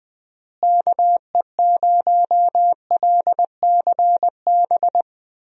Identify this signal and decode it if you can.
Morse code, heard as a keyed tone, about 20 wpm, 700 Hz. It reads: KE0LCB
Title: USA Call Signs - 6 Characters - Repeated 3x - 20wpm